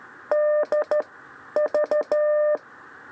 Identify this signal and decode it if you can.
Morse code, heard as a keyed tone, about 13 wpm, spaced as for 9 wpm, 600 Hz. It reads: DV
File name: 6 lezione cw 2